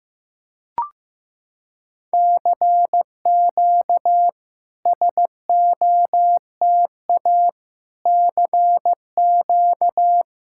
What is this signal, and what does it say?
Morse code, heard as a keyed tone, about 15 wpm, 700 Hz. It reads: CQ SOTA CQ
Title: SOTA QSO Practice 15wpm